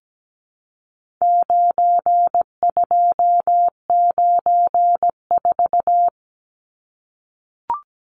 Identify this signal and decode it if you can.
Morse code, heard as a keyed tone, about 17 wpm, 700 Hz. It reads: 9294
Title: Numbers - 4-Digit 17wpm